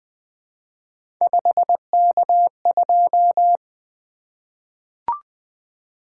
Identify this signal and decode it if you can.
Morse code, heard as a keyed tone, about 20 wpm, 700 Hz. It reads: 5K2
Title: Three Letter-Number Combinations - 20wpm